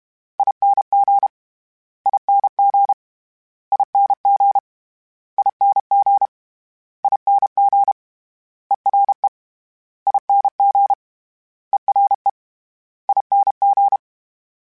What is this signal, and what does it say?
Morse code, heard as a keyed tone, about 32 wpm, 800 Hz. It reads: ING ING ING ING ING ERE ING ERE ING
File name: Special INGs / Special INGs-mp3s / Ere ing @30